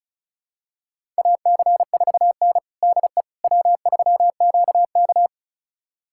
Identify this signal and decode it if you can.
Morse code, heard as a keyed tone, about 34 wpm, 700 Hz. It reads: AC4N DE W3QK
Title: USA Callsign DE Callsign - 4 Characters - 35wpm